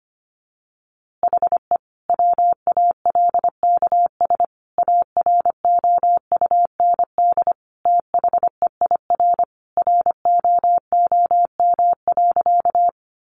Morse code, heard as a keyed tone, about 25 wpm, 700 Hz. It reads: HE WALKS AROUND THEIR ROOM.
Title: Easy Phrases 25wpm